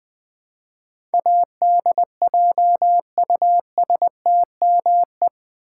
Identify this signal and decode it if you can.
Morse code, heard as a keyed tone, about 20 wpm, 700 Hz. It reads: ADJUSTME
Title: Build A Word - 4 Letters At A Time - 10-Letter Words - 20wpm